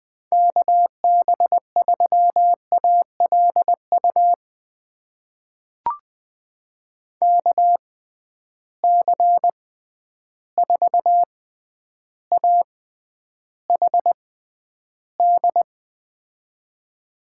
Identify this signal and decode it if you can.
Morse code, heard as a keyed tone, about 20 wpm, spaced as for 10 wpm, 700 Hz. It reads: KB3ALU KC4AHD
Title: USA Call Signs - 6x Character Spacing - 20wpm